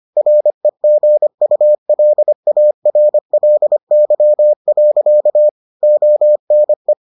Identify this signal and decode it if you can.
Morse code, heard as a keyed tone, about 25 wpm, 600 Hz. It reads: REGULARLY. ONE